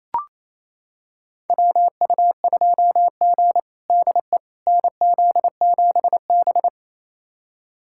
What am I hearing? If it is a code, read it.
Morse code, heard as a keyed tone, about 28 wpm, 700 Hz. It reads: WU2G DE NZ7B
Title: USA Callsign DE Callsign - 4 Characters - 28wpm